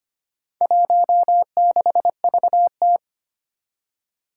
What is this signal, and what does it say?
Morse code, heard as a keyed tone, about 25 wpm, 700 Hz. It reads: 16VT
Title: Four Challenging Letter-Number Combinations 25wpm